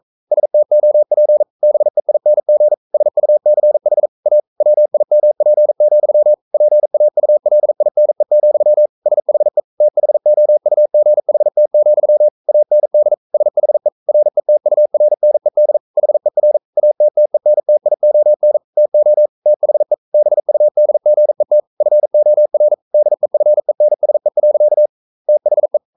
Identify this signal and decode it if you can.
Morse code, heard as a keyed tone, about 42 wpm, 600 Hz. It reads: STOP BEING SUCH A WIMP, PAULINE, SHE THOUGHT, AND SHE RETURNED HER ATTENTION TO THE BUDGET FOR DEFENSE. THE
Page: Romanen Never af Ken Follet på engelsk.